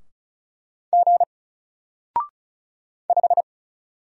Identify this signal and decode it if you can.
Morse code, heard as a keyed tone, about 36 wpm, 700 Hz. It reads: G 5
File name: Single-Letter-Number-35wpm